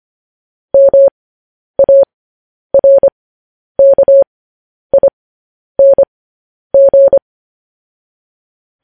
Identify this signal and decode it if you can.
Morse code, heard as a keyed tone, about 25 wpm, spaced as for 10 wpm, 550 Hz. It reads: MARKING
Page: News Headlines in Morse Code at 10 WPM